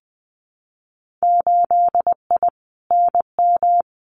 Morse code, heard as a keyed tone, about 20 wpm, 700 Hz. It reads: 8I NM
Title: Winter Field Day Exchanges Encore 20wpm